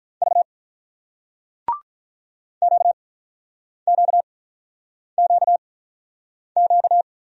Morse code, heard as a keyed tone, about 43 wpm, spaced as for 8 wpm, 700 Hz. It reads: V QQQQ
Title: Speed-Racing Single Letters 35wpm